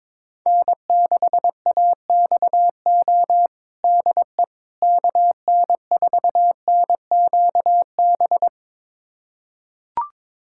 Morse code, heard as a keyed tone, about 22 wpm, 700 Hz. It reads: N6AXO DE KN4NQB
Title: USA Callsign DE Callsign 22wpm